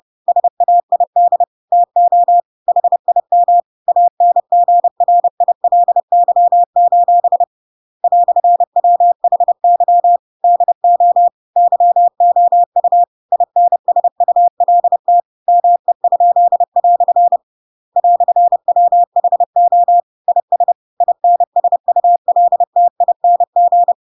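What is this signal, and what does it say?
Morse code, heard as a keyed tone, about 30 wpm, 700 Hz. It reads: SAID TO HIM ANGRILY: "WHY DO YOU INSULT ME?" "WHO IS INSULTING
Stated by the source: Chapter 2 of The Adventures of Pinocchio read as a book in Morse code (CW) at 30 wpm, by ik1hge.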